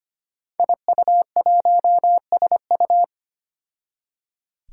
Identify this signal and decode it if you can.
Morse code, heard as a keyed tone, about 25 wpm, 700 Hz. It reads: IU1SU
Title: Five Challenging Letter-Number Combinations 25wpm